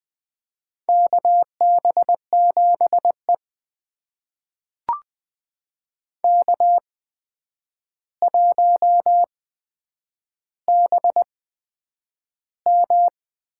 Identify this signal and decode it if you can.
Morse code, heard as a keyed tone, about 20 wpm, spaced as for 9 wpm, 700 Hz. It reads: KB7E K1BM
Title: USA Call Signs - 8x Character Spacing - 20wpm